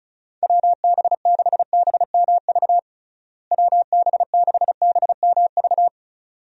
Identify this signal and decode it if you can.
Morse code, heard as a keyed tone, about 36 wpm, 700 Hz. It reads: WB6BMV WB6BMV
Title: USA Call Signs - 6 Characters - Repeated 2x - 35wpm